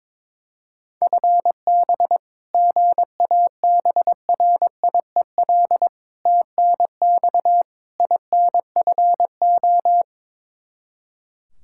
Morse code, heard as a keyed tone, about 22 wpm, 700 Hz. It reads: FB GABRIEL TNX INFO
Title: Practice QSO - Exchange 3 - More Info - 22wpm